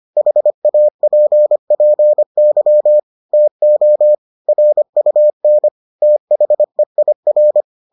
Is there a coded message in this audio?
Yes: Morse code from 0.2 to 7.6 s, heard as a keyed tone, about 25 wpm, 600 Hz. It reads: HAPPY TO RUN THEIR